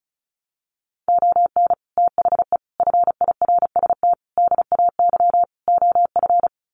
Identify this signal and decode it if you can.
Morse code, heard as a keyed tone, about 34 wpm, 700 Hz. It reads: ON THE FIRST DAY OF
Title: Sets of 5 Words Encore 35wpm